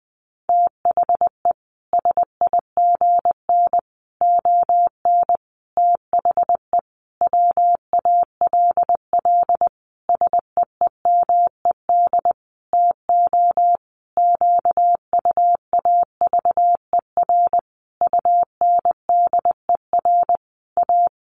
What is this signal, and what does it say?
Morse code, heard as a keyed tone, about 20 wpm, 700 Hz. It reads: THE SIGN ON THE WALL SEEMED TO QUAVER UNDER A
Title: A Sound Of Thunder 20wpm